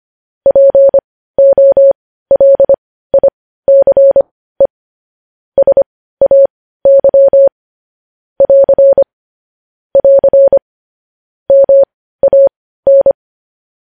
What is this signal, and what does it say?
Morse code, heard as a keyed tone, about 25 wpm, spaced as for 15 wpm, 550 Hz. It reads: POLICE SAY <AR> <AR> MAN
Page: news-headlines-in-morse-code-at-15-wpm